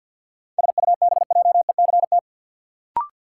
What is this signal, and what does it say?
Morse code, heard as a keyed tone, about 51 wpm, 700 Hz. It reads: SUBJECT
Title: Sentences from Top 800 Words - New Word Drill - 50wpm